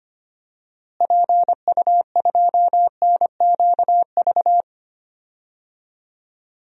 Morse code, heard as a keyed tone, about 25 wpm, 700 Hz. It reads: PU2NQV
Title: DXCC-100 Call Signs - 6 Characters - Repeated 3x - 25wpm